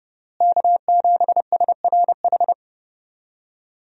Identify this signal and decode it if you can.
Morse code, heard as a keyed tone, about 31 wpm, 700 Hz. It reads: K7SRH
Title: USA Call Signs - 5 Characters - Repeated 3x - 30wpm